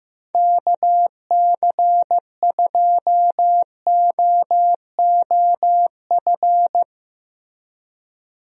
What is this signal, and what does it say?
Morse code, heard as a keyed tone, about 16 wpm, 700 Hz. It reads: KC2OOF
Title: USA Call Signs - 6 Characters - Repeated 3x - 15wpm